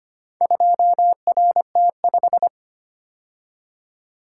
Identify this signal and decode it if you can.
Morse code, heard as a keyed tone, about 25 wpm, 700 Hz. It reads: 2RT5
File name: practice / 25wpm / Four Letter-Number Combinations - 25wpm